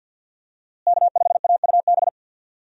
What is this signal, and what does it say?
Morse code, heard as a keyed tone, about 51 wpm, 700 Hz. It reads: K5AUB
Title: USA Call Signs - 5 Characters - Repeated 2x - 50wpm